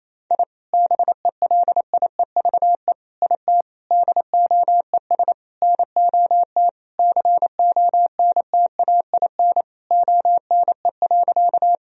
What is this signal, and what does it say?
Morse code, heard as a keyed tone, about 27 wpm, 700 Hz. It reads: I BELIEVE IT DOES NOT CONTAIN ONE.
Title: Sentences from Top 1000 Words - Review Part 2 - 28wpm